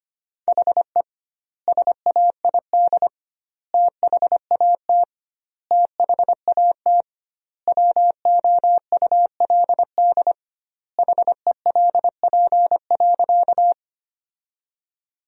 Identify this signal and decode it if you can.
Morse code, heard as a keyed tone, about 25 wpm, 700 Hz. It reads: HE SAID THAT THAT WOULD HELP.
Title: 2x Word-Spacing - Sentences from Top 200 Words - 25wpm